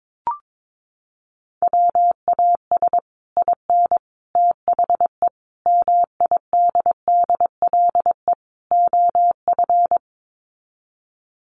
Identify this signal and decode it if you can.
Morse code, heard as a keyed tone, about 22 wpm, 700 Hz. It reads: WAS IN THE MIDDLE OF
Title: Sets of 5 Words Encore 22wpm